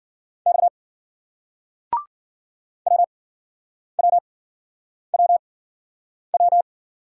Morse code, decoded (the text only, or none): X WWWW